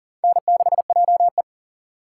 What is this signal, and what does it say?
Morse code, heard as a keyed tone, about 40 wpm, 700 Hz. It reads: N6JE
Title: USA Call Signs - 2x Character Spacing - 40wpm